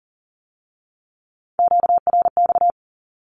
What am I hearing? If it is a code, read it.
Morse code, heard as a keyed tone, about 38 wpm, 700 Hz. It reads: QRX
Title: Q-Codes with Spoken Meaning - 40wpm